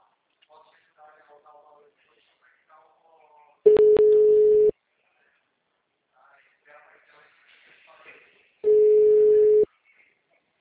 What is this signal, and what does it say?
Ringback tone (Europe, 425 Hz), with speech.